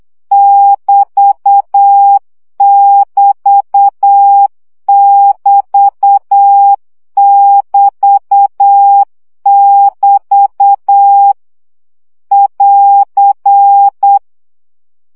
Morse code, heard as a keyed tone, about 8 wpm, 800 Hz. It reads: <BT><BT><BT><BT><BT> <AR>
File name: CW gyakorlas egyenlo 7 baud